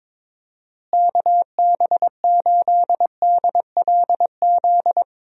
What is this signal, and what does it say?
Morse code, heard as a keyed tone, about 22 wpm, 700 Hz. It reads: KB8DLZ
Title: USA Call Signs - 6 Characters - Repeated 3x - 22wpm